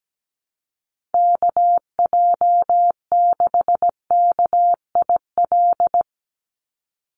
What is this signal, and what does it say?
Morse code, heard as a keyed tone, about 17 wpm, 700 Hz. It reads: KJ6KIL
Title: USA Call Signs 17wpm